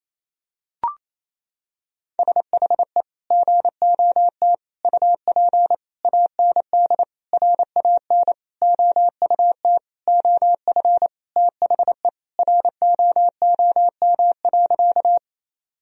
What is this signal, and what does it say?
Morse code, heard as a keyed tone, about 28 wpm, 700 Hz. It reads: SHE GOT UP AND RAN OUT OF THE ROOM.
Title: Sentences from Top 1000 Words - Review Part 3 - 28wpm